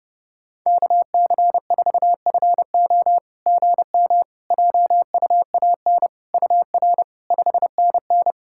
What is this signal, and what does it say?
Morse code, heard as a keyed tone, about 30 wpm, 700 Hz. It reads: KC4FO GM JUAN UR 5NN
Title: SOTA QSO Practice 30wpm